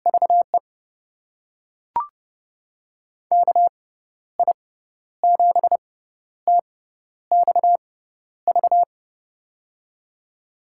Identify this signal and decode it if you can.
Morse code, heard as a keyed tone, about 31 wpm, spaced as for 12 wpm, 700 Hz. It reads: VE KI7TXV
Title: USA Call Signs - 6 Characters - 6x Character Spacing - 30wpm